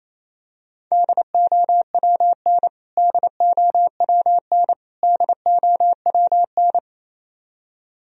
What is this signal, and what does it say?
Morse code, heard as a keyed tone, about 28 wpm, 700 Hz. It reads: DOWN DOWN DOWN
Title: Top 100 Words Repeated 28wpm